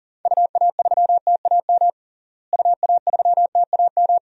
Morse code, heard as a keyed tone, about 40 wpm, 700 Hz. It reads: UA3TAM UA3TAM
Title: DXCC-100 Call Signs - 6 Characters - Repeated 2x - 40wpm